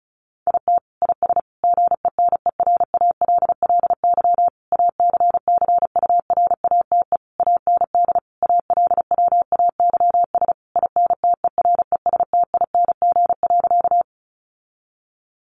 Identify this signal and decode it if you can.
Morse code, heard as a keyed tone, about 34 wpm, 700 Hz. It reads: IT IS GENERALLY ACCURATE AND ALWAYS INTERESTING.
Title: Sentences from Top 1200 Words 35wpm